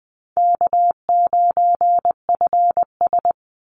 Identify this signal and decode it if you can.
Morse code, heard as a keyed tone, about 20 wpm, 700 Hz. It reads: K9FS
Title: Easy Call Signs - 20wpm